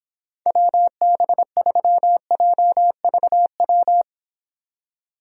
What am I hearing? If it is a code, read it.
Morse code, heard as a keyed tone, about 26 wpm, 700 Hz. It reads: WB3JVW